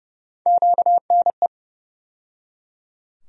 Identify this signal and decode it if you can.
Morse code, heard as a keyed tone, about 31 wpm, 700 Hz. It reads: QNE
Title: USA Call Sign Postfixes - 3 Letters 30wpm